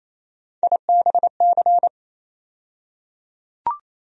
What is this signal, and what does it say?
Morse code, heard as a keyed tone, about 28 wpm, 700 Hz. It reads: IBC
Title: USA Call Sign Postfixes - 3 Letters 28wpm